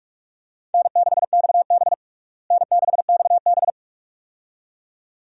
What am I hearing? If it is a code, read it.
Morse code, heard as a keyed tone, about 46 wpm, 700 Hz. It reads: N6XB N6XB
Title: USA Call Signs - 4 Characters - Repeated 2x - 45wpm